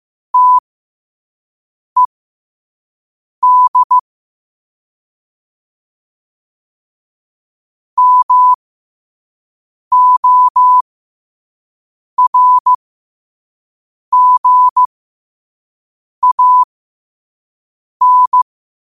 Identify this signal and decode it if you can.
Morse code, heard as a keyed tone, about 15 wpm, spaced as for 5 wpm, 1000 Hz. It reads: TED MORGAN
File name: qotdfiles / 5wpm